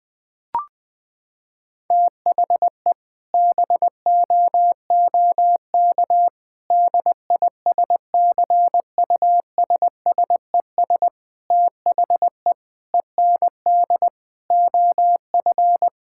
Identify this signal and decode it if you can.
Morse code, heard as a keyed tone, about 20 wpm, 700 Hz. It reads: THE BOOK DISCUSSES THE END OF
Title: Sentences with 6-Word Phrases - 20wpm